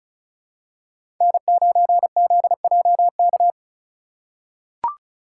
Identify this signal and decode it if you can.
Morse code, heard as a keyed tone, about 36 wpm, 700 Hz. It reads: N9ZJK
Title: USA Call Signs - 5 Characters - 4x Character Spacing - 35wpm